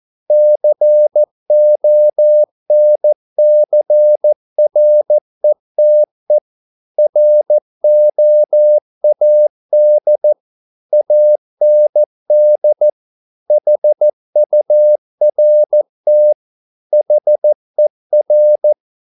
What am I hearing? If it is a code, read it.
Morse code, heard as a keyed tone, about 14 wpm, 600 Hz. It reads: CONCRETE ROAD AND HURT HER